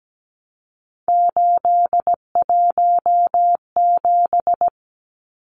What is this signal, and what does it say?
Morse code, heard as a keyed tone, about 17 wpm, 700 Hz. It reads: 817
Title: Numbers - 3-Digit 17wpm